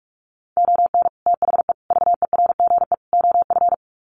Morse code, heard as a keyed tone, about 44 wpm, 700 Hz. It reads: ON THE VERGE OF
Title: Sets of 4 Words 45wpm